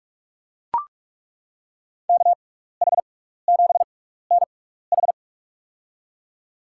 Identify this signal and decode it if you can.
Morse code, heard as a keyed tone, about 42 wpm, spaced as for 15 wpm, 700 Hz. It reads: KH7NH
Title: USA Call Signs - 6x Character Spacing - 45wpm